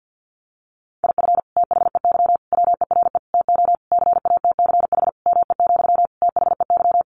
Morse code, heard as a keyed tone, about 50 wpm, 700 Hz. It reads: IF THEY WERE TO CATCH ME, THEY
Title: Sentences from Top 1500 Words 50wpm